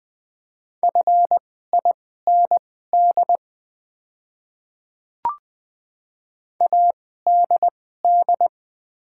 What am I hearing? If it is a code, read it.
Morse code, heard as a keyed tone, about 20 wpm, spaced as for 14 wpm, 700 Hz. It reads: FIND ADD
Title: 2x Character Spacing - Top 200 Words - 20wpm